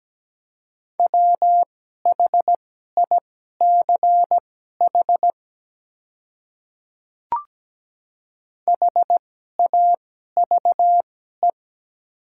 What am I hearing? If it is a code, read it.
Morse code, heard as a keyed tone, about 17 wpm, spaced as for 12 wpm, 700 Hz. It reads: WHICH HAVE
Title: 2x Character Spacing - Top 300 Words - 17wpm